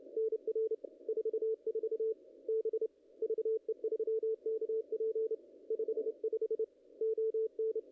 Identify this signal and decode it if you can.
Morse code, heard as a keyed tone, about 28 wpm, 440 Hz. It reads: NR 44 B VE3KP 55 ON
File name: contests / ss-cw-k5tr-op-n5rz -0000